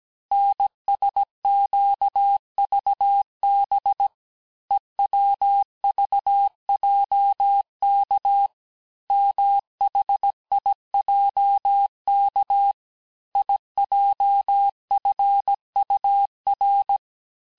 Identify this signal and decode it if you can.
Morse code, heard as a keyed tone, about 17 wpm, 800 Hz. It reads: NSQVB EWVJK MHIJK IJFUR
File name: cours-du-lundi files / cours-1587-texte-chiffre-vitesse-16MPM